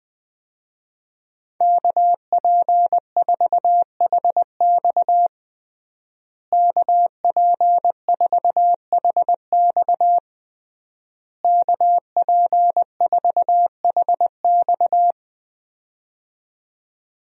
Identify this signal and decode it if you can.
Morse code, heard as a keyed tone, about 21 wpm, 700 Hz. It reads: KP4HX KP4HX KP4HX
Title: USA Call Signs - 5 Characters - Repeated 3x - 20wpm